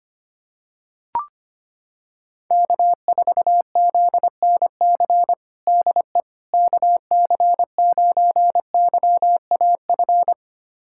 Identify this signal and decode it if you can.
Morse code, heard as a keyed tone, about 25 wpm, 700 Hz. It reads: K4ZNC DE KC9YAF
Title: USA Callsign DE Callsign 25wpm